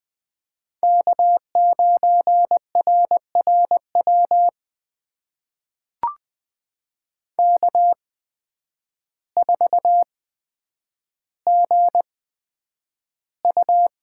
Morse code, heard as a keyed tone, about 20 wpm, spaced as for 9 wpm, 700 Hz. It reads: K9RRW K4GU
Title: USA Call Signs - 8x Character Spacing - 20wpm